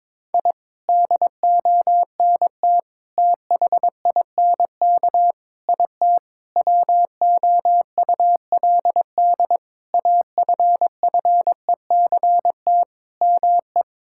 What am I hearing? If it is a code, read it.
Morse code, heard as a keyed tone, about 22 wpm, 700 Hz. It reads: I DONT THINK IT WOULD AFFECT ME
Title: Sentences from Top 1000 Words - Review Part 6 - 22wpm